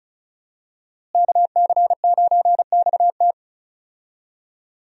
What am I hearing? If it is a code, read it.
Morse code, heard as a keyed tone, about 36 wpm, 700 Hz. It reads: KC9XT
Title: USA Call Signs - 5 Characters - Repeated 2x - 35wpm